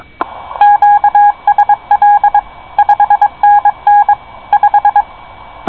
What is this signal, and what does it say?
Morse code, heard as a keyed tone, about 22 wpm, 820 Hz. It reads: QSL 5NN 5
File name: co8tw / cqwwcw2010